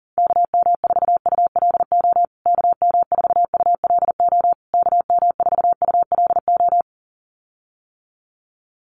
Morse code, heard as a keyed tone, about 40 wpm, 700 Hz. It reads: KM4ULO KM4ULO KM4ULO
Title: USA Call Signs Repeated 3x 40wpm